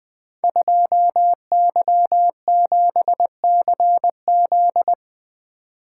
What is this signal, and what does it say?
Morse code, heard as a keyed tone, about 20 wpm, 700 Hz. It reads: 2Y7CZ
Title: Code Groups - 5 Letter:Numbers - 20wpm